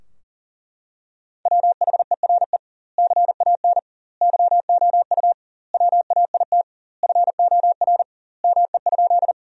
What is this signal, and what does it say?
Morse code, heard as a keyed tone, about 40 wpm, 700 Hz. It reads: WHERE CAN YOU WAIT FOR ME?
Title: 2x Word-Spacing - Sentences from Top 500 Words - 40wpm